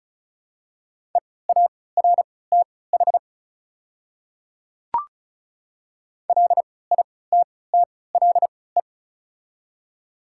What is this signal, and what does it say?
Morse code, heard as a keyed tone, about 36 wpm, spaced as for 20 wpm, 700 Hz. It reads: EARTH LITTLE
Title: 3x Character Spacing - Top 200 Words - 35wpm